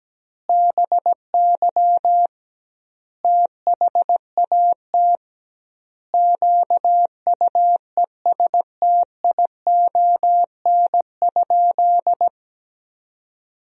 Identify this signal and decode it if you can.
Morse code, heard as a keyed tone, about 17 wpm, 700 Hz. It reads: BY THAT QUESTION?
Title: 2x Word-Spacing - Sentences from Top 400 Words - 17wpm